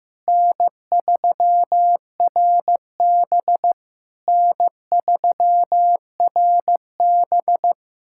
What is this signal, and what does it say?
Morse code, heard as a keyed tone, about 15 wpm, 700 Hz. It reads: N3RB N3RB
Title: SOTA QSO Practice 15wpm